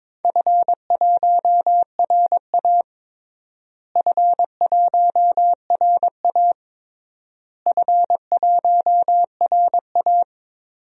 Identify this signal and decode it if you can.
Morse code, heard as a keyed tone, about 22 wpm, 700 Hz. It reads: F1RA F1RA F1RA
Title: DXCC-100 Call Signs - 4 Characters - Repeated 3x - 22wpm